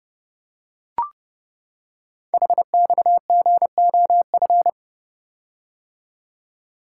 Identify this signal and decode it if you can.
Morse code, heard as a keyed tone, about 30 wpm, 700 Hz. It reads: HXGOF
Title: Code Groups - 5 Letters - 30wpm